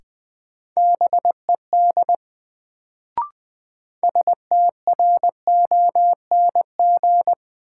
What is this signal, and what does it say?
Morse code, heard as a keyed tone, about 20 wpm, 700 Hz. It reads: BED STRONG
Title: Sentences from Top 500 Words - New Word Drill - 20wpm